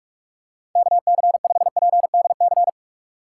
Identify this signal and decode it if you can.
Morse code, heard as a keyed tone, about 46 wpm, 700 Hz. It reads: KC5PDC
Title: USA Call Signs - 6 Characters - 3x Character Spacing - 45wpm